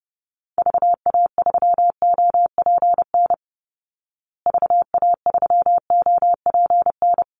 Morse code, heard as a keyed tone, about 30 wpm, 700 Hz. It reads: VA3OPN VA3OPN
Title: Call Signs Repeated 3x - 30wpm